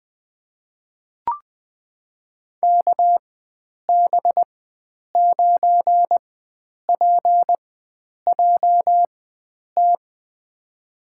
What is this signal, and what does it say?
Morse code, heard as a keyed tone, about 20 wpm, spaced as for 9 wpm, 700 Hz. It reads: KB9PJT
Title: USA Call Signs - 6 Characters - 4x Character Spacing - 20wpm